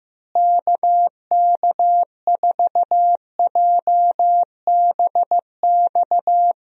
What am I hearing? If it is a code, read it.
Morse code, heard as a keyed tone, about 15 wpm, 700 Hz. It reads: KK4JBX
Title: USA Call Signs - 6 Characters - Repeated 2x - 15wpm